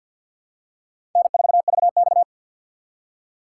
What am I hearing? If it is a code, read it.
Morse code, heard as a keyed tone, about 50 wpm, 700 Hz. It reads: N4VX
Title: USA Call Signs - 4 Characters - 2x Character Spacing - 50wpm